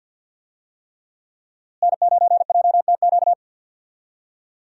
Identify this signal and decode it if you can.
Morse code, heard as a keyed tone, about 47 wpm, 700 Hz. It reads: N9JTQ
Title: USA Call Signs - 5 Characters - Repeated 3x - 50wpm